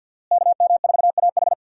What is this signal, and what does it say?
Morse code, heard as a keyed tone, about 51 wpm, 700 Hz. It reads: KM4AH
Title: USA Call Signs - 5 Characters - Repeated 2x - 50wpm